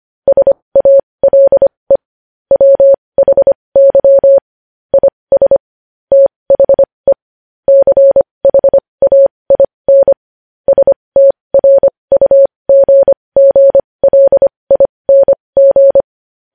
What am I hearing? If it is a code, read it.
Morse code, heard as a keyed tone, about 25 wpm, spaced as for 20 wpm, 550 Hz. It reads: SALE WHY IS THE CHAIN STRUGGLING
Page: News headlines in Morse code at 20 WPM